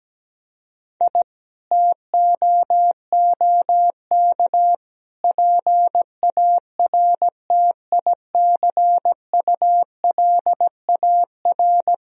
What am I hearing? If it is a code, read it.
Morse code, heard as a keyed tone, about 17 wpm, 700 Hz. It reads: I TOOK PARTICULAR
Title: Sentences from Top 1000 Words - Review Part 4 - 17wpm